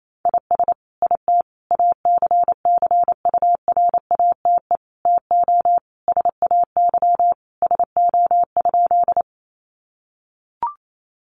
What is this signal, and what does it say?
Morse code, heard as a keyed tone, about 28 wpm, 700 Hz. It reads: IS IT ACCURATE TO SAY SO?
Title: Sentences from Top 1200 Words 28wpm